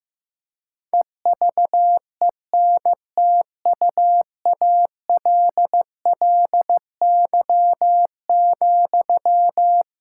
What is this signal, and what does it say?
Morse code, heard as a keyed tone, about 15 wpm, 700 Hz. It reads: EVENTUALLY,
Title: Sentences from Top 1000 Words - Review Part 6 - 15wpm